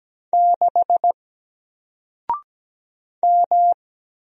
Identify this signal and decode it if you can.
Morse code, heard as a keyed tone, about 17 wpm, 700 Hz. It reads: 6 M
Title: Single Character (Letter:Number:Punctuation:Prosign) Alt 17wpm